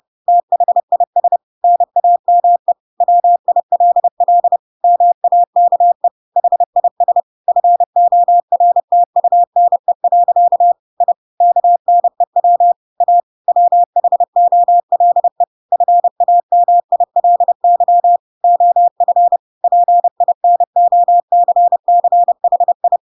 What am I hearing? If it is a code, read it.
Morse code, heard as a keyed tone, about 30 wpm, 700 Hz. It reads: THIS NAME WILL MAKE HIS FORTUNE. I KNEW A WHOLE FAMILY OF PINOCCHI
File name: Pinocchio / Pinocchio ch3 30 30 ik1hge